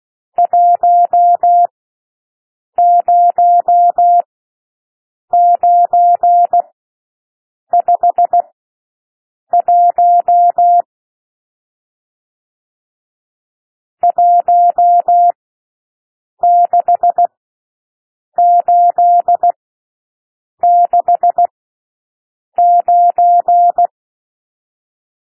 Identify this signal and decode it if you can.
Morse code, heard as a keyed tone, about 16 wpm, spaced as for 6 wpm, 700 Hz. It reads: 10951 16869
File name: SOUND NUMBERS 16 wpm